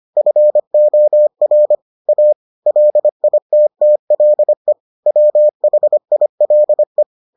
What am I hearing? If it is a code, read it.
Morse code, heard as a keyed tone, about 25 wpm, 600 Hz. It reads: FOR A LITTLE WHILE